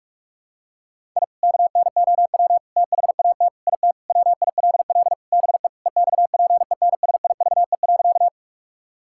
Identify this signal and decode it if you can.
Morse code, heard as a keyed tone, about 46 wpm, 700 Hz. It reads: I KNOW THAT IT WILL BE EXPENSIVE.
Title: Sentences from Top 1000 Words - Review Part 6 - 45wpm